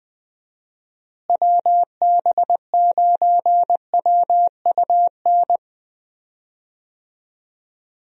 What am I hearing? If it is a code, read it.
Morse code, heard as a keyed tone, about 20 wpm, 700 Hz. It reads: WB9WUN
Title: USA Call Signs - 6 Characters - Repeated 2x - 20wpm